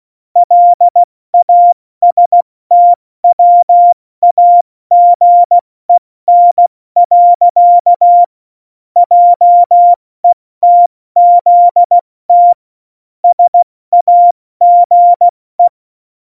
Morse code, heard as a keyed tone, about 16 wpm, 700 Hz. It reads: LASTWAGEN. JETZT SAGE
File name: Morsekurs / Witze / Witz 25. Februar 2020, 70 F80